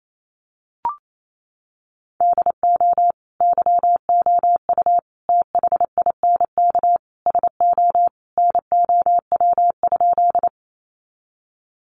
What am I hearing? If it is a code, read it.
Morse code, heard as a keyed tone, about 28 wpm, 700 Hz. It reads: DO YOU THINK SO NOW?
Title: Sentences from Top 200 Words 28wpm